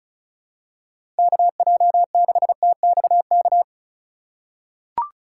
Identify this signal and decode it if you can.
Morse code, heard as a keyed tone, about 34 wpm, 700 Hz. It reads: KJ6TXK
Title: USA Call Signs - 6 Characters - 3x Character Spacing - 35wpm